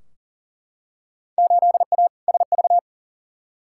Morse code, heard as a keyed tone, about 37 wpm, 700 Hz. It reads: 8A SV
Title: Field Day Exchanges Encore - 40wpm